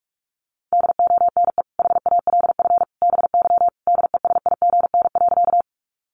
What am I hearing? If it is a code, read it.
Morse code, heard as a keyed tone, about 44 wpm, 700 Hz. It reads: DONE HALF BY DESIGN.
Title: Sentences from Top 800 Words 45wpm